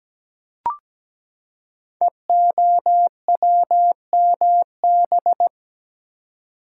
Morse code, heard as a keyed tone, about 17 wpm, 700 Hz. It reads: EOWMB
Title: Code Groups - 5 Letter:Numbers - 17wpm